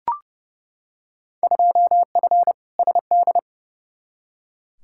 Morse code, heard as a keyed tone, about 31 wpm, 700 Hz. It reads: 2F SD
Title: Field Day Exchanges Encore - 30wpm